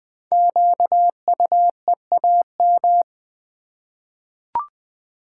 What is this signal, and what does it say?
Morse code, heard as a keyed tone, about 20 wpm, 700 Hz. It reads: QUEAM
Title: Code Groups - 5 Letters - 20wpm